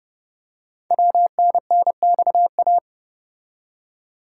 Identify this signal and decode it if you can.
Morse code, heard as a keyed tone, about 30 wpm, 700 Hz. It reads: WNNXA
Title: Code Groups - 5 Letters - 30wpm